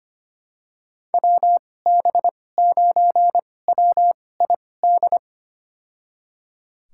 Morse code, heard as a keyed tone, about 25 wpm, spaced as for 18 wpm, 700 Hz. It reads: WB9WID
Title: USA Call Signs - 2x Character Spacing - 25wpm